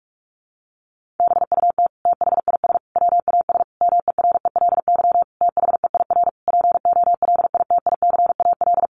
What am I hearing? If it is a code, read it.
Morse code, heard as a keyed tone, about 44 wpm, 700 Hz. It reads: BUT THIS WAS MERELY THEIR POLITICAL
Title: Sentences from Top 1400 Words 45wpm